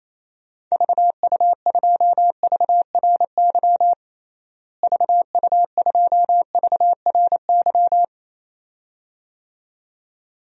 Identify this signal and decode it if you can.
Morse code, heard as a keyed tone, about 28 wpm, 700 Hz. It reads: VU2VRY VU2VRY
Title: DXCC-100 Call Signs - 6 Characters - Repeated 2x - 28wpm